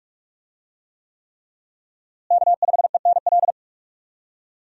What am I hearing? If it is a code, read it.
Morse code, heard as a keyed tone, about 45 wpm, 700 Hz. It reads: K5ENL
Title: USA Call Signs - 5 Characters - Repeated 3x - 45wpm